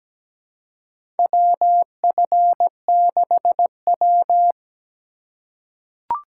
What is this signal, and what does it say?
Morse code, heard as a keyed tone, about 17 wpm, 700 Hz. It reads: WF6W
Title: USA Call Signs - 4 Characters - 2x Character Spacing - 17wpm